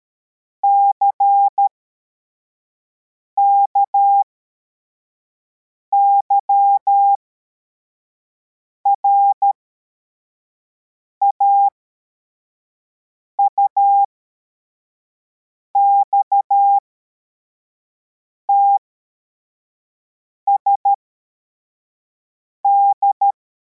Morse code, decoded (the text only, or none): CKYRA UXTSD